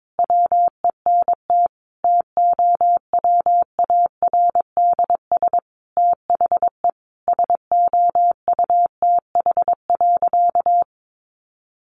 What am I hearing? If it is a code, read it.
Morse code, heard as a keyed tone, about 22 wpm, 700 Hz. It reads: WENT TOWARDS THE SOUTH.
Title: Sentences from Top 1100 Words 22wpm